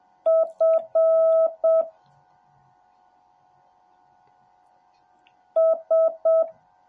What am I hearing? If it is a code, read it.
Morse code, heard as a keyed tone, about 7 wpm, 650 Hz. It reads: F S